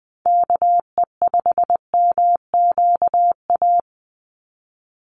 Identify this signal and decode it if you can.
Morse code, heard as a keyed tone, about 20 wpm, 700 Hz. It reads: KE5MQA
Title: USA Call Signs 20wpm